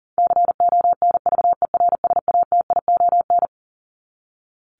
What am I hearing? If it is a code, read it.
Morse code, heard as a keyed tone, about 40 wpm, 700 Hz. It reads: CONVERSATION
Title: Top 1751-2000 Words 40wpm